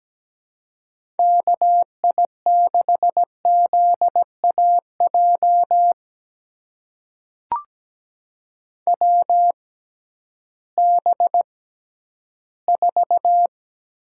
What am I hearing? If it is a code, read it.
Morse code, heard as a keyed tone, about 17 wpm, spaced as for 11 wpm, 700 Hz. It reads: KI6ZAJ WB4
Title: USA Call Signs - 6 Characters - 6x Character Spacing - 17wpm